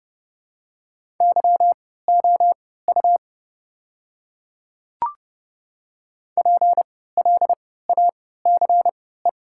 Morse code, heard as a keyed tone, about 30 wpm, spaced as for 17 wpm, 690 Hz. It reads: YOU PLACE
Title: 3x Character Spacing - Top 500 Words - 30wpm